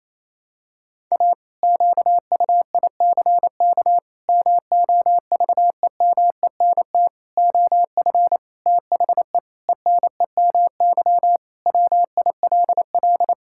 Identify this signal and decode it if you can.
Morse code, heard as a keyed tone, about 27 wpm, 700 Hz. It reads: A QUICK MOVEMENT OF THE ENEMY WILL
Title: Pangram Challenge v2 - 28wpm